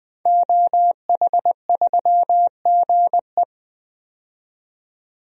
Morse code, heard as a keyed tone, about 20 wpm, 700 Hz. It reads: OH3GE
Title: DXCC-100 Call Signs - 5 Characters - 20wpm